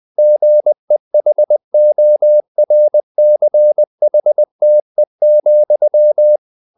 Morse code, heard as a keyed tone, about 20 wpm, 600 Hz. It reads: GEHORCHTE,